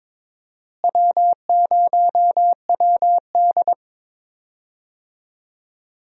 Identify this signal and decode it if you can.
Morse code, heard as a keyed tone, about 21 wpm, 700 Hz. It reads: W0WD
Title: USA Call Signs - 4 Characters - Repeated 2x - 22wpm